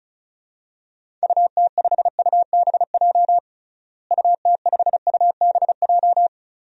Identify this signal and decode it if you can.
Morse code, heard as a keyed tone, about 36 wpm, 700 Hz. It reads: UT5UBJ UT5UBJ
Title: DXCC-100 Call Signs - 6 Characters - Repeated 3x - 35wpm